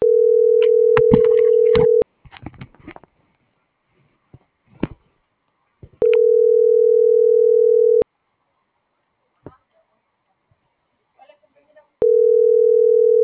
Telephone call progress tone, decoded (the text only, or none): ringback tone